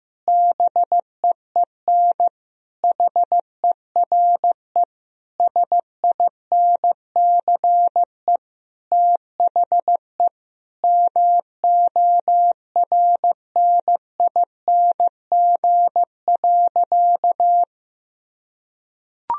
Morse code, heard as a keyed tone, about 15 wpm, 700 Hz. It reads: BEEN HERE SINCE THE MORNING.
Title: Sentences from Top 1000 Words - Review Part 1 - 15wpm